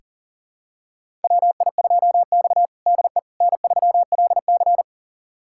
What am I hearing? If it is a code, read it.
Morse code, heard as a keyed tone, about 40 wpm, 700 Hz. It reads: WI2X DE N3LC
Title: USA Callsign DE Callsign - 4 Characters - 40wpm